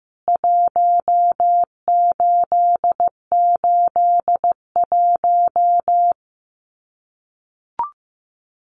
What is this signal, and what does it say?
Morse code, heard as a keyed tone, about 15 wpm, 700 Hz. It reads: 1881
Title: CWT Exchange 15wpm